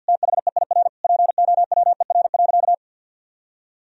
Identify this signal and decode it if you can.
Morse code, heard as a keyed tone, about 50 wpm, 700 Hz. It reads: THEIR POWER.
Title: Sentences from Top 1000 Words - Review Part 2 - 50wpm